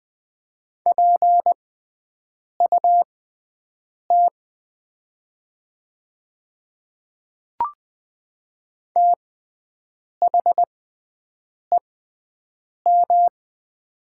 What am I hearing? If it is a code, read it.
Morse code, heard as a keyed tone, about 20 wpm, spaced as for 7 wpm, 700 Hz. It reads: PUT THEM
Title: 6x Character Spacing - Top 300 Words - 20wpm